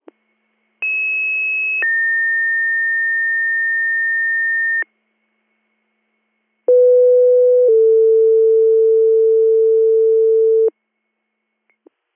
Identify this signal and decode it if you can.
Two-tone page, heard as a one-second tone, then a three-second tone, about 2600 then 1800 Hz, about 500 then 450 Hz. It is heard from 0.8 to 10.7 s.